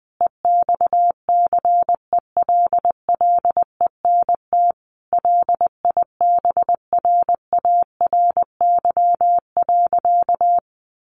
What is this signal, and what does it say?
Morse code, heard as a keyed tone, about 20 wpm, 700 Hz. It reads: EXCELLENT LIBRARY.
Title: Sentences from Top 1000 Words 20wpm